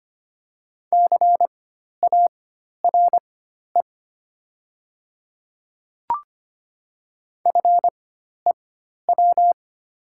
Morse code, heard as a keyed tone, about 25 wpm, spaced as for 12 wpm, 700 Hz. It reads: CARE FEW
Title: 4x Character Spacing - Top 300 Words - 25wpm